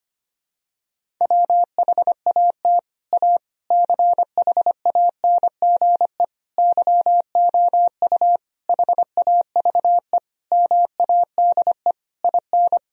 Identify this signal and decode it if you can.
Morse code, heard as a keyed tone, about 25 wpm, 700 Hz. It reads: WHAT A CHANGE YOU HAVE MADE IN
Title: Sentences from Top 1000 Words - Review Part 1 - 25wpm